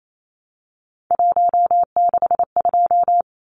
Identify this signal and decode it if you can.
Morse code, heard as a keyed tone, about 28 wpm, 700 Hz. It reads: 162
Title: Numbers - 3-Digit 28wpm